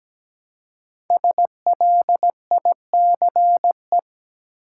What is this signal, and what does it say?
Morse code, heard as a keyed tone, about 17 wpm, 700 Hz. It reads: SLICE
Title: Five Letter Words - 17wpm